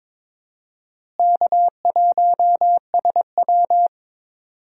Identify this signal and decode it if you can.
Morse code, heard as a keyed tone, about 23 wpm, 700 Hz. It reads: K1SW
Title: USA Call Signs - 4 Characters - Repeated 3x - 22wpm